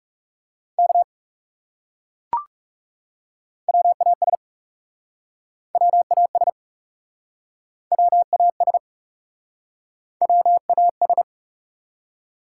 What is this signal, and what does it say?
Morse code, heard as a keyed tone, about 37 wpm, 700 Hz. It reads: K WAS WAS WAS WAS
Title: Speed-Racing Top 180 QSO Elements 30wpm